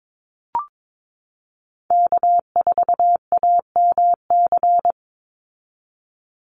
Morse code, heard as a keyed tone, about 22 wpm, 700 Hz. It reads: K4AMC
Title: Call Signs Repeated 2x - 22wpm